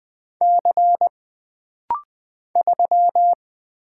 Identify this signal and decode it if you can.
Morse code, heard as a keyed tone, about 20 wpm, 700 Hz. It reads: C 3
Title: Single Letter-Number Alt - Rapid-Fire 20wpm